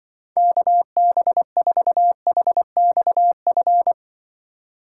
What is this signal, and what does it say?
Morse code, heard as a keyed tone, about 24 wpm, 700 Hz. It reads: KB4HXF
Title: Speed Racing - USA 2x3 Call Signs 20wpm